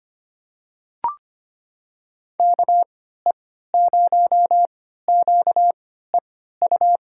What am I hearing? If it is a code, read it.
Morse code, heard as a keyed tone, about 25 wpm, spaced as for 14 wpm, 700 Hz. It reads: KE0QEU
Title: USA Call Signs - 6 Characters - 3x Character Spacing - 25wpm